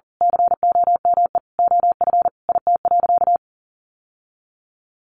Morse code, heard as a keyed tone, about 40 wpm, 700 Hz. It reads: COME OF IT.
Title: Sentences from Top 1100 Words 40wpm